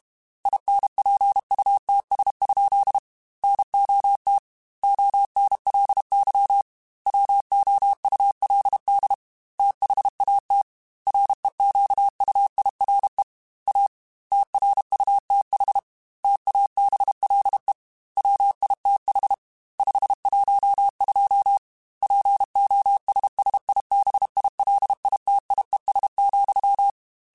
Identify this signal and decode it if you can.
Morse code, heard as a keyed tone, about 32 wpm, 800 Hz. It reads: INPUTS? NOT ONLY WOULD THAT REQUIRE A TRUTH TABLE WITH 512 POSSIBILITIES,
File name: Technical Subjects / Technical Subjects-mp3s / AND Gate @30